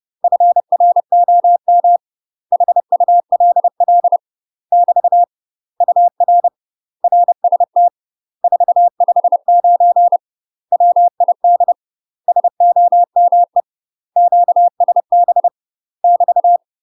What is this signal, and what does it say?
Morse code, heard as a keyed tone, about 30 wpm, 700 Hz. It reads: FROM HULL <BT> UR RST 459 WID SOME QSB <BT>
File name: UK MASTER QSOs TRAINING / 30WPM / UK MASTER QSOs 0044